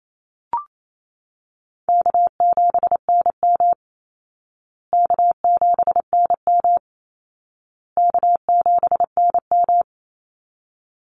Morse code, heard as a keyed tone, about 28 wpm, 700 Hz. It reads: K7NM K7NM K7NM
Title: Call Signs Repeated 2x - 28wpm